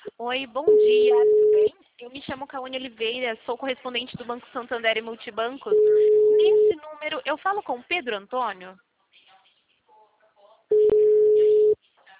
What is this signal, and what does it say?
Ringback tone (Europe, 425 Hz), with speech.
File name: out- -2019- - - .98